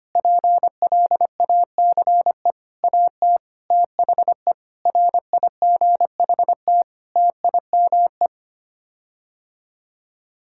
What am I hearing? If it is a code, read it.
Morse code, heard as a keyed tone, about 25 wpm, 700 Hz. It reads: PLACE AT THE RIGHT TIME
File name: Sets of 9 Words - 25wpm